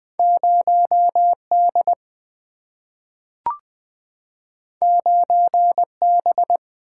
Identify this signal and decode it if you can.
Morse code, heard as a keyed tone, about 20 wpm, 700 Hz. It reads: 0 D 9 B
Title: Single-Letter Single-Number 20wpm